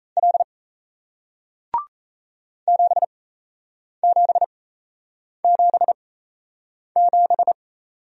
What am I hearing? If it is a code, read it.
Morse code, heard as a keyed tone, about 38 wpm, spaced as for 8 wpm, 700 Hz. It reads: L 7777